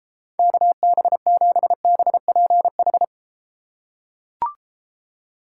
Morse code, heard as a keyed tone, about 31 wpm, 700 Hz. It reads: KB7BPH